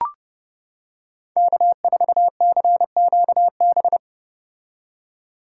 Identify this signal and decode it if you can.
Morse code, heard as a keyed tone, about 30 wpm, 700 Hz. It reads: K4CQB